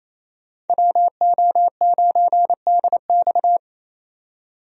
Morse code, heard as a keyed tone, about 28 wpm, 700 Hz. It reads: WO9DX
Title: USA Call Signs - 5 Characters - 28wpm